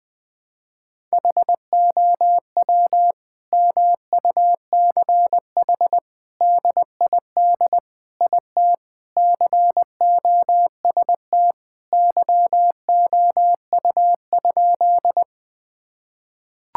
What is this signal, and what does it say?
Morse code, heard as a keyed tone, about 20 wpm, 700 Hz. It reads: HOW MUCH DID IT COST YOU?
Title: Sentences from Top 1000 Words - Review Part 5 - 20wpm